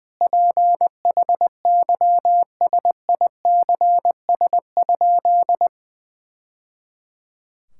Morse code, heard as a keyed tone, about 20 wpm, 700 Hz. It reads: PHYSICS?
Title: Sentences from Top 1000 Words - Review Part 6 - 20wpm